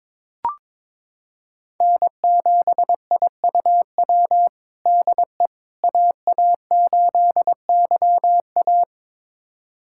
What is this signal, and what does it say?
Morse code, heard as a keyed tone, about 22 wpm, 700 Hz. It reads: N7IUW DE AA8YA
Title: USA Callsign DE Callsign - 5 Characters - 22wpm